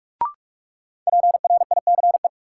Morse code, heard as a keyed tone, about 46 wpm, 700 Hz. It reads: PRICE
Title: Sentences from Top 600 Words - New Word Drill - 45wpm